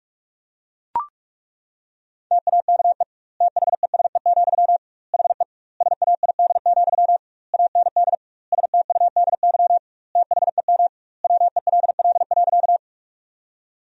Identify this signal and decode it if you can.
Morse code, heard as a keyed tone, about 41 wpm, 700 Hz. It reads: TAKE THESE, HE SAID, AND STUDY THEM WELL.
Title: 2x Word-Spacing - Sentences from Top 300 Words - 45wpm